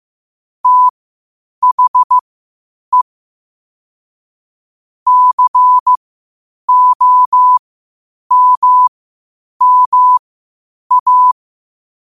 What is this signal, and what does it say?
Morse code, heard as a keyed tone, about 15 wpm, spaced as for 8 wpm, 1000 Hz. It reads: THE COMMA
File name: qotdfiles / 8wpm